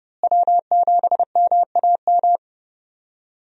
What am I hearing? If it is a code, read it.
Morse code, heard as a keyed tone, about 30 wpm, 700 Hz. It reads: W7MAM
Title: USA Call Signs - 5 Characters - Repeated 3x - 30wpm